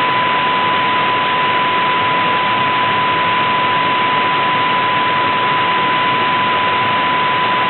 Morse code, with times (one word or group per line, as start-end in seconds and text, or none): none